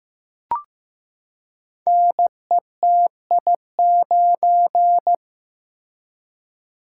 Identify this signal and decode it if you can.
Morse code, heard as a keyed tone, about 15 wpm, 700 Hz. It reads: NETI9
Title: Code Groups - 5 Letter:Numbers - 15wpm